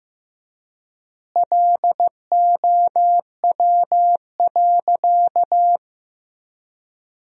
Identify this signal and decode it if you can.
Morse code, heard as a keyed tone, about 15 wpm, 700 Hz. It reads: LOW.
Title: 3x Word-Spacing - Sentences from Top 500 Words - 15wpm